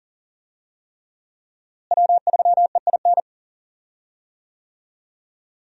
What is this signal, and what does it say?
Morse code, heard as a keyed tone, about 40 wpm, 700 Hz. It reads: W3EIN
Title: USA Call Signs - 5 Characters - Repeated 3x - 40wpm